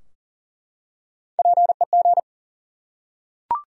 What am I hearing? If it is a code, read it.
Morse code, heard as a keyed tone, about 40 wpm, 700 Hz. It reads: PEG
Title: USA Common Names - Max 3 Letters - 40wpm